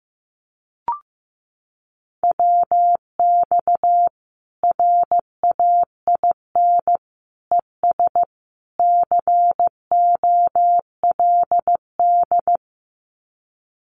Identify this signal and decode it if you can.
Morse code, heard as a keyed tone, about 15 wpm, 700 Hz. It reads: WX RAIN ES COLD
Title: Common QSO Phrases 15wpm